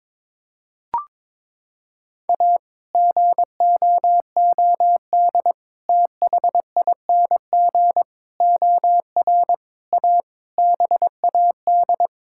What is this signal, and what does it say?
Morse code, heard as a keyed tone, about 22 wpm, 700 Hz. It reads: A GOOD THING OR A BAD
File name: practice / 22wpm / Sets of 7 Words - 22wpm